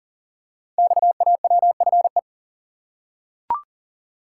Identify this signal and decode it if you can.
Morse code, heard as a keyed tone, about 42 wpm, 700 Hz. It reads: XAWFE
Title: Code Groups - 5 Letters - 40wpm